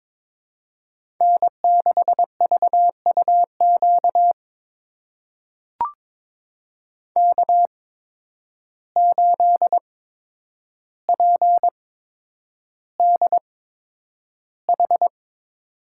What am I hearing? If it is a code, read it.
Morse code, heard as a keyed tone, about 22 wpm, spaced as for 9 wpm, 700 Hz. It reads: N6VUQ K8PDH
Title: USA Call Signs - 5 Characters - 8x Character Spacing - 22wpm